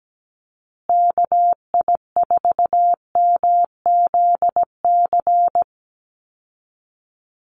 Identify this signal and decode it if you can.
Morse code, heard as a keyed tone, about 17 wpm, 700 Hz. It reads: KI4MZC
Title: Call Signs Repeated 2x - 17wpm